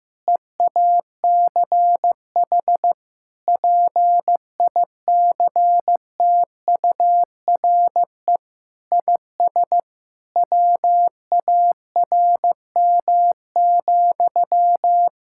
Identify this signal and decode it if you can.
Morse code, heard as a keyed tone, about 14 wpm, 700 Hz. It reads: EACH PICTURE IS WARM,
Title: Sentences from Top 1600 Words 15wpm